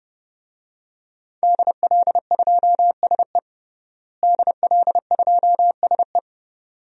Morse code, heard as a keyed tone, about 30 wpm, 700 Hz. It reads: DL2SE DL2SE
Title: DXCC-100 Call Signs - 5 Characters - Repeated 2x - 30wpm